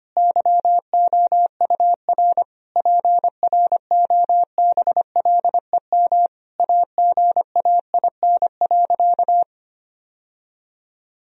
Morse code, heard as a keyed tone, about 25 wpm, 700 Hz. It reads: YOUR PROBLEM AGAIN.
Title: Sentences from Top 1000 Words - Review Part 3 - 25wpm